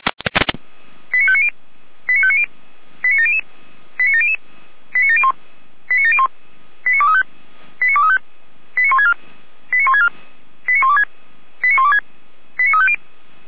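Five-tone selective call (ZVEI), heard as quick runs of five tones, tones of about 70 ms. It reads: 89590 89590 89700 89700 89701 89701 89235 89235 89164 89164 89126 89126 89350
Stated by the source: ZVEI five-tone selective call 89092 received by SDR.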